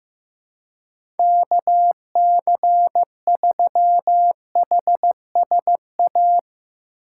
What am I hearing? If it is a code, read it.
Morse code, heard as a keyed tone, about 15 wpm, 700 Hz. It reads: KC3HSA
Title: USA Call Signs - 6 Characters - 2x Character Spacing - 15wpm